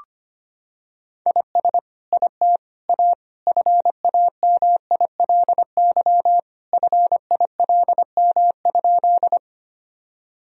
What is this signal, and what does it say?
Morse code, heard as a keyed tone, about 25 wpm, 700 Hz. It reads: IS IT A FAMILY FILM?
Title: Sentences from Top 1000 Words - Review Part 3 - 25wpm